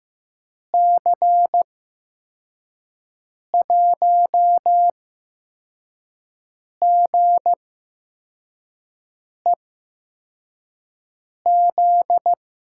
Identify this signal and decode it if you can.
Morse code, heard as a keyed tone, about 15 wpm, spaced as for 4 wpm, 700 Hz. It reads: C1GEZ
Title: USA Call Signs - 6 Characters - 8x Character Spacing - 15wpm